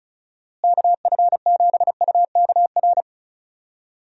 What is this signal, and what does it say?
Morse code, heard as a keyed tone, about 36 wpm, 700 Hz. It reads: KF7UKR
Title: USA Call Signs - 6 Characters - Repeated 2x - 35wpm